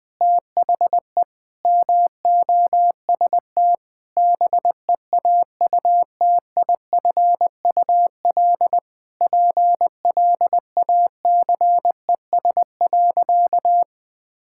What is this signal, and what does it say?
Morse code, heard as a keyed tone, about 20 wpm, 700 Hz. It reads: THE MOST BEAUTIFUL PLACES.
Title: Sentences with 3-Word Phrases - 20wpm